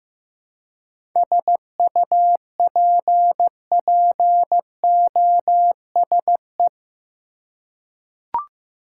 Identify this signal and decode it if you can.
Morse code, heard as a keyed tone, about 15 wpm, 700 Hz. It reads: SUPPOSE
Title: Seven Letter Words - 15wpm